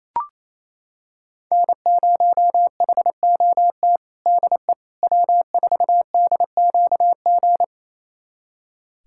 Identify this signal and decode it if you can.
Morse code, heard as a keyed tone, about 28 wpm, 700 Hz. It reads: N0HOT DE W4DQG
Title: USA Callsign DE Callsign - 5 Characters - 28wpm